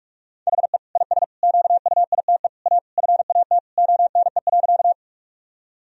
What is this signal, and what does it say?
Morse code, heard as a keyed tone, about 46 wpm, 700 Hz. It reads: HE IS QUITE A FAT ONE.
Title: Sentences from Top 1000 Words - Review Part 4 - 45wpm